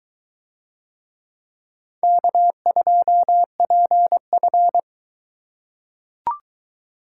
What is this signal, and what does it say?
Morse code, heard as a keyed tone, about 22 wpm, 700 Hz. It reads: K2PF